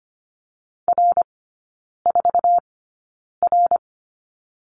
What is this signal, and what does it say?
Morse code, heard as a keyed tone, about 25 wpm, spaced as for 9 wpm, 700 Hz. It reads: R4R
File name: Lesson 14e - Number 2 - Three Characters 25wpm